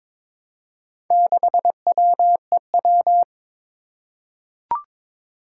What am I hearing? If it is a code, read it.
Morse code, heard as a keyed tone, about 22 wpm, 700 Hz. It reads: 6WEW
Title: Four Challenging Letter-Number Combinations 22wpm